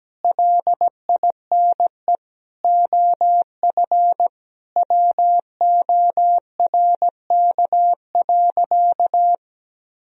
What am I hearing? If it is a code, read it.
Morse code, heard as a keyed tone, about 17 wpm, 700 Hz. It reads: LINE OF WORK.
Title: Sentences from Top 1000 Words - Review Part 4 - 17wpm